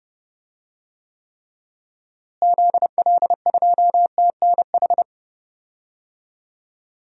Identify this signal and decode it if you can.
Morse code, heard as a keyed tone, about 30 wpm, 700 Hz. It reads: ZL2TNH
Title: DXCC-100 Call Signs - 6 Characters - Repeated 2x - 30wpm